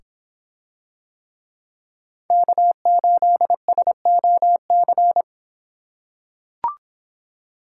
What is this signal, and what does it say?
Morse code, heard as a keyed tone, about 25 wpm, 700 Hz. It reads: K8SOC